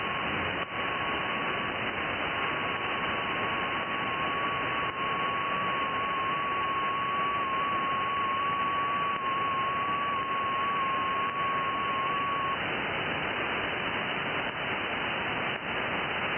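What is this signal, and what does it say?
FT8.
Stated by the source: FT8 signal received on an IC-7100 radio.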